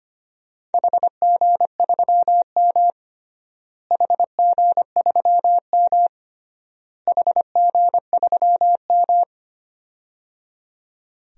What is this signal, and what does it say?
Morse code, heard as a keyed tone, about 25 wpm, 700 Hz. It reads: HG3M HG3M HG3M
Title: DXCC-100 Call Signs - 4 Characters - Repeated 3x - 25wpm